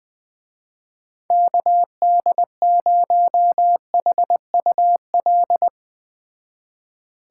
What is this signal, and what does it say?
Morse code, heard as a keyed tone, about 20 wpm, 700 Hz. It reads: KD0HUL
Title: Speed Racing - USA 2x3 Call Signs 17wpm